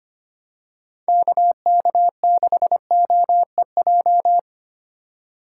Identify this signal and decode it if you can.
Morse code, heard as a keyed tone, about 25 wpm, 700 Hz. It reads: KK6OEJ
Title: USA Call Signs - 6 Characters - 4x Character Spacing - 25wpm